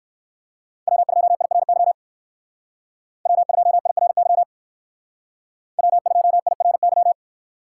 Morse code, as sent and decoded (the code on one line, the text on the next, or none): .-- ..--- .. .-. -.-- / .-- ..--- .. .-. -.-- / .-- ..--- .. .-. -.--
W2IRY W2IRY W2IRY